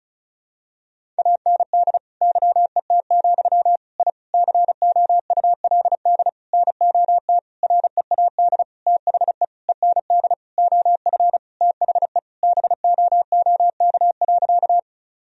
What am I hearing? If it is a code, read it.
Morse code, heard as a keyed tone, about 34 wpm, 700 Hz. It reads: AND YET, I COULD NOT READ THE END OF THE BOOK.
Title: Sentences from Top 1000 Words - Review Part 1 - 35wpm